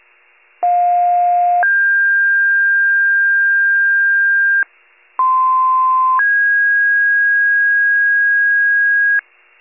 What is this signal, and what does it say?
Two-tone page, heard as a one-second tone, then a three-second tone, about 710 then 1600 Hz, about 1040 then 1650 Hz. It is heard from 0.6 to 9.2 s.
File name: wcalert